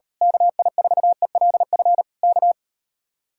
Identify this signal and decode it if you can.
Morse code, heard as a keyed tone, about 39 wpm, 700 Hz. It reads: KI4ELF K